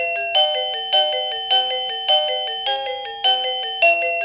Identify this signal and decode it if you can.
There is no code.